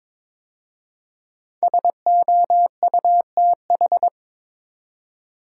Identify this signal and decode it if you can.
Morse code, heard as a keyed tone, about 22 wpm, 700 Hz. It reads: SOUTH
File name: 5 Letter Words - First 3 Letters Given - 22wpm